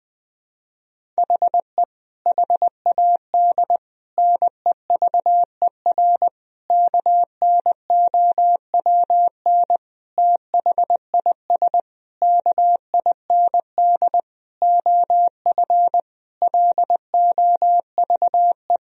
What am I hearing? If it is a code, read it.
Morse code, heard as a keyed tone, about 20 wpm, 700 Hz. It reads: HE HAD NEVER KNOWN THIS KIND OF LOVE
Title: Sentences from Top 1000 Words - Review Part 3 - 20wpm